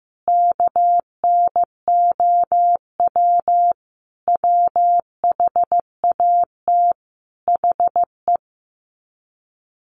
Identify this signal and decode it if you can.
Morse code, heard as a keyed tone, about 15 wpm, 700 Hz. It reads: KNOW WHAT HE
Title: Sets of 5 Words Encore2 15wpm